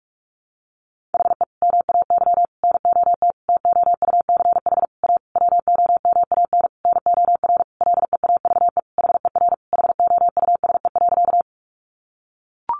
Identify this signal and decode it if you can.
Morse code, heard as a keyed tone, about 44 wpm, 700 Hz. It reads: HE MAY NOT TOUCH A WOMAN NOR LEAVE HER HOUSE.
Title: Sentences from Top 1500 Words 45wpm